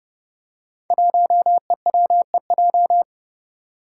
Morse code, heard as a keyed tone, about 30 wpm, 700 Hz. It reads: 1EWEJ
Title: Five Challenging Letter-Number Combinations 30wpm